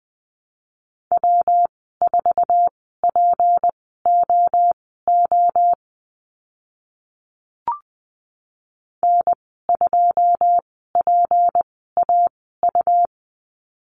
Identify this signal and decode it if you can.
Morse code, heard as a keyed tone, about 20 wpm, spaced as for 14 wpm, 700 Hz. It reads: W4POO N2PAU
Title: 154 Lesson 25g - Letter p - Call Signs 20wpm